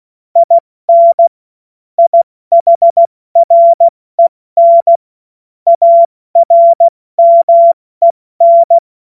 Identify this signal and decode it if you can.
Morse code, heard as a keyed tone, about 16 wpm, 680 Hz. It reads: IN IHREN ARMEN
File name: Morsekurs / Witze / Witz 05 Dezember 2023 70 F80